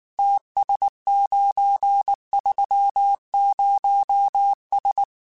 Morse code, heard as a keyed tone, about 19 wpm, 800 Hz. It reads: TS930S
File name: QSOs / QSOs-mp3s / WA7QOX 2@18